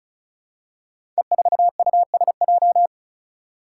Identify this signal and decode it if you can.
Morse code, heard as a keyed tone, about 36 wpm, 700 Hz. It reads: E4USJ
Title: Five Challenging Letter-Number Combinations 35wpm